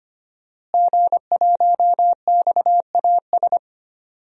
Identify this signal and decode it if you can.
Morse code, heard as a keyed tone, about 25 wpm, 700 Hz. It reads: G1XAS
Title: DXCC-100 Call Signs - 5 Characters - Repeated 3x - 25wpm